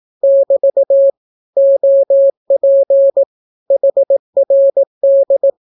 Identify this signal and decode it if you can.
Morse code, heard as a keyed tone, about 18 wpm, 550 Hz. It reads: <BT> OP HRD
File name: naukapodstaw / skroty / skroty